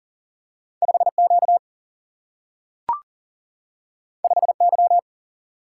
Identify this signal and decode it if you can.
Morse code, heard as a keyed tone, about 39 wpm, 700 Hz. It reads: 5Q 5Y
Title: Single-Letter Single-Number 40wpm